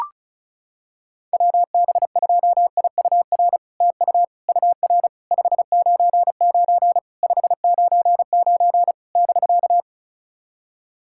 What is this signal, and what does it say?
Morse code, heard as a keyed tone, about 36 wpm, 700 Hz. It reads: WB2IUR TU UR 599 599 <BK>
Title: SOTA QSO Practice 35wpm